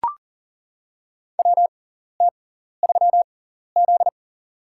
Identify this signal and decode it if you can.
Morse code, heard as a keyed tone, about 40 wpm, spaced as for 14 wpm, 700 Hz. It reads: WT3Z
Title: USA Call Signs - 6x Character Spacing - 40wpm